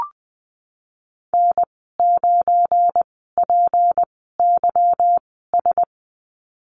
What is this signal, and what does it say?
Morse code, heard as a keyed tone, about 20 wpm, spaced as for 15 wpm, 700 Hz. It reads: N9PYS
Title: 186 Lesson 30g - Letter b - Call Signs 20wpm